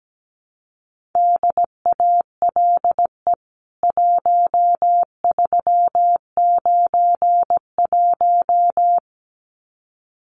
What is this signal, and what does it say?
Morse code, heard as a keyed tone, about 17 wpm, 700 Hz. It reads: DALE 1391
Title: CWT Exchange 17wpm